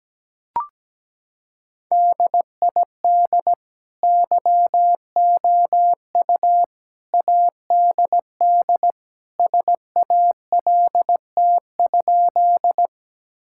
Morse code, heard as a keyed tone, about 17 wpm, 700 Hz. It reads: DID YOU ADD SALT?
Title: Sentences from Top 1000 Words - Review Part 5 - 17wpm